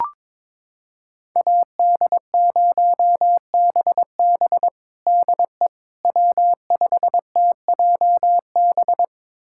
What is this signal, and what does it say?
Morse code, heard as a keyed tone, about 22 wpm, 700 Hz. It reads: AD0BB DE W5TJB
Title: USA Callsign DE Callsign - 5 Characters - 22wpm